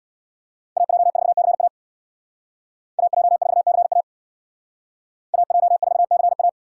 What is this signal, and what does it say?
Morse code, heard as a keyed tone, about 60 wpm, 700 Hz. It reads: AJ4CA AJ4CA AJ4CA
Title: Speed Racing - USA 2x2 Call Signs 50wpm